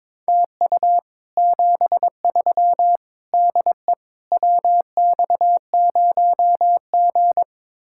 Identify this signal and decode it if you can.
Morse code, heard as a keyed tone, about 22 wpm, 700 Hz. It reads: TU 73 DE WX0G
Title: Speed-Racing Common QSO Phrases - 22wpm